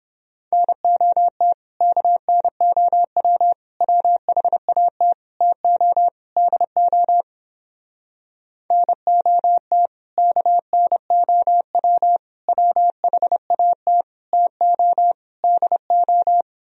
Morse code, heard as a keyed tone, about 31 wpm, 700 Hz. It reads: NOT KNOW WHAT TO DO NOT KNOW WHAT TO DO